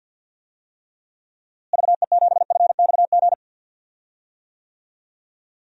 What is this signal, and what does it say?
Morse code, heard as a keyed tone, about 50 wpm, 700 Hz. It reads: VE7FXG
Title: DXCC-100 Call Signs - 6 Characters - Repeated 2x - 50wpm